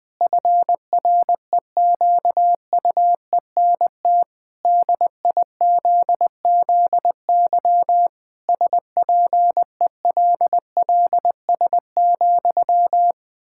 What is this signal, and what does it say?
Morse code, heard as a keyed tone, about 20 wpm, 700 Hz. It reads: FREQUENT DIZZY SPELLS,
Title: Pangram Challenge v2 - 20wpm